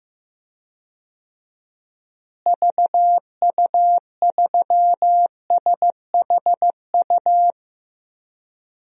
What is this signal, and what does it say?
Morse code, heard as a keyed tone, about 15 wpm, 700 Hz. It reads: VU3SHU
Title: DXCC-100 Call Signs - 6 Characters - Repeated 2x - 15wpm